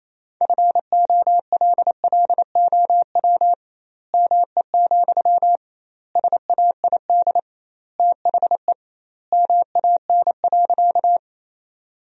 Morse code, heard as a keyed tone, about 27 wpm, 700 Hz. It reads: FOLLOW ME, SAID THE MAN.
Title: 2x Word-Spacing - Sentences from Top 200 Words - 28wpm